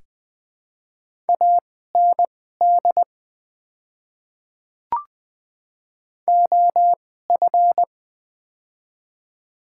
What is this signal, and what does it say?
Morse code, heard as a keyed tone, about 20 wpm, spaced as for 14 wpm, 700 Hz. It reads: AND OF
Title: 2x Character Spacing - Top 100 Words - 20wpm